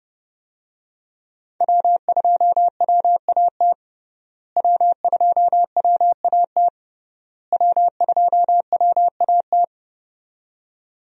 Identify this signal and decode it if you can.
Morse code, heard as a keyed tone, about 30 wpm, 700 Hz. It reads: W2WAT W2WAT W2WAT
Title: USA Call Signs - 5 Characters - Repeated 3x - 30wpm